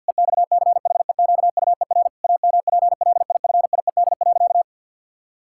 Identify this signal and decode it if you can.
Morse code, heard as a keyed tone, about 50 wpm, 700 Hz. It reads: EXCHEQUER AMPLIFIED.
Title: Pangram Challenge v2 - 50wpm